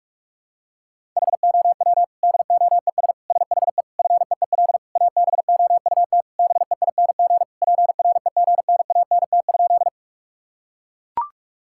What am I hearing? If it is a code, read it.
Morse code, heard as a keyed tone, about 41 wpm, 700 Hz. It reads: HOW DOES SHE FEEL ABOUT BEING PREGNANT?
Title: Sentences from Top 1000 Words - Review Part 6 - 45wpm